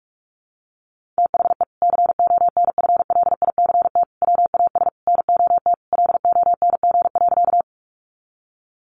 Morse code, heard as a keyed tone, about 44 wpm, 700 Hz. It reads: THE CONFLICT WAS NOT LONG.
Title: Sentences from Top 1300 Words 45wpm